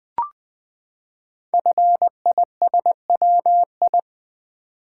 Morse code, heard as a keyed tone, about 20 wpm, 700 Hz. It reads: FISWI
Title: Code Groups - 5 Letters - 20wpm